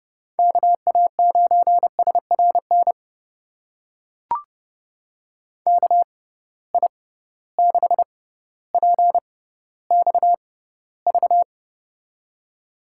Morse code, heard as a keyed tone, about 30 wpm, spaced as for 15 wpm, 700 Hz. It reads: KA9SRN KI6PXV
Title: USA Call Signs - 6 Characters - 6x Character Spacing - 30wpm